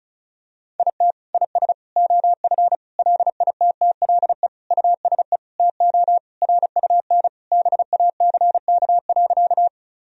Morse code, heard as a keyed tone, about 36 wpm, 700 Hz. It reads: IT IS OF LITTLE USE TO RUN BACK.
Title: Sentences from Top 1000 Words - Review Part 1 - 35wpm